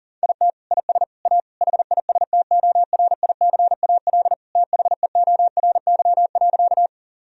Morse code, heard as a keyed tone, about 40 wpm, 700 Hz. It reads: IT IS A HISTORICAL THEORY.
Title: Sentences from Top 1000 Words - Review Part 4 - 40wpm